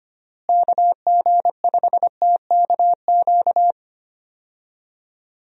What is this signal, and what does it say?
Morse code, heard as a keyed tone, about 25 wpm, 700 Hz. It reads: KG5TKQ
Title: USA Call Signs - 6 Characters - Repeated 2x - 25wpm